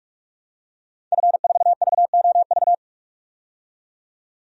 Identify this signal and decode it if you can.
Morse code, heard as a keyed tone, about 46 wpm, 700 Hz. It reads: F4VOV
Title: DXCC-100 Call Signs - 5 Characters - 45wpm